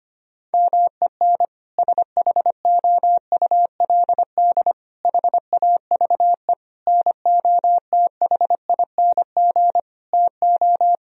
Morse code, heard as a keyed tone, about 25 wpm, 700 Hz. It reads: MEN SHOULD HAVE NOTHING TO
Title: Sentences from Top 1000 Words - Review Part 3 - 25wpm